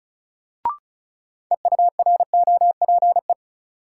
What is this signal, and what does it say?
Morse code, heard as a keyed tone, about 36 wpm, 700 Hz. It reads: EUROPE
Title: Sets of 5 Words Encore - New Word Drill - 35wpm